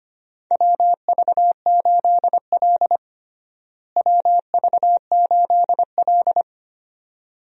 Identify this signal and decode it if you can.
Morse code, heard as a keyed tone, about 25 wpm, 700 Hz. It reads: WV8L WV8L
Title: USA Call Signs - 4 Characters - Repeated 2x - 25wpm